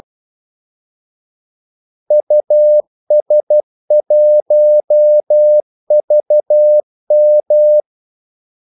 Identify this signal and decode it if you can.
Morse code, heard as a keyed tone, about 12 wpm, 600 Hz. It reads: US1VM
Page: 2 ekstra ordmellemrum mellem kaldesignalerne